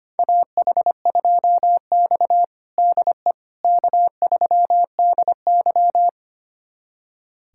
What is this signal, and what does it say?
Morse code, heard as a keyed tone, about 25 wpm, 700 Hz. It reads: AH2X DE K3DY
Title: USA Callsign DE Callsign - 4 Characters - 25wpm